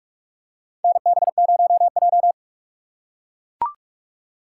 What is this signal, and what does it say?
Morse code, heard as a keyed tone, about 43 wpm, 700 Hz. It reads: NB0J
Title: USA Call Signs - 2x Character Spacing - 45wpm